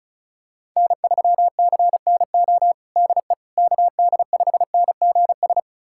Morse code, heard as a keyed tone, about 36 wpm, 700 Hz. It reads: N3CNO DE KD5NGS
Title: USA Callsign DE Callsign 35wpm